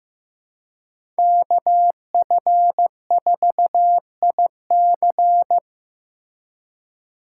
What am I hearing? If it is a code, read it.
Morse code, heard as a keyed tone, about 16 wpm, 700 Hz. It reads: KF4IC
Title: USA Call Signs - 5 Characters - 4x Character Spacing - 15wpm